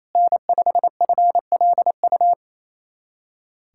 Morse code, heard as a keyed tone, about 28 wpm, 700 Hz. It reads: N5FLU
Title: USA Callsign DE Callsign - 5 Characters - 28wpm